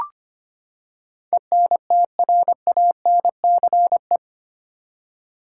Morse code, heard as a keyed tone, about 25 wpm, 700 Hz. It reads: ENTRANCE
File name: Eight Letter Words - 25wpm